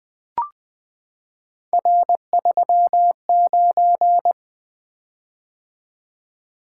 Morse code, heard as a keyed tone, about 20 wpm, 700 Hz. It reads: R39
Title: Three Letter-Number Combinations - 20wpm